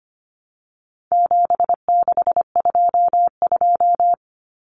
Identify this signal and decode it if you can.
Morse code, heard as a keyed tone, about 25 wpm, 700 Hz. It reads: 7622
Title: Common Cut Numbers - 4-Digit 25wpm